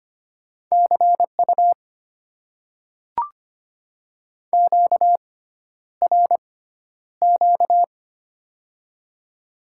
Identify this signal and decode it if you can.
Morse code, heard as a keyed tone, about 25 wpm, spaced as for 11 wpm, 700 Hz. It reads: CU QRQ
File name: Top 90 QSO Elements - 6x Character Spacing - 25wpm